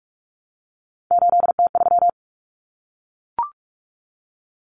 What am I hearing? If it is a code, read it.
Morse code, heard as a keyed tone, about 44 wpm, 700 Hz. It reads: 8T3
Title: Common Cut Numbers - 3-Digit 45wpm